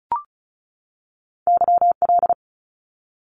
Morse code, heard as a keyed tone, about 34 wpm, 700 Hz. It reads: YL
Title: Abbreviations 35wpm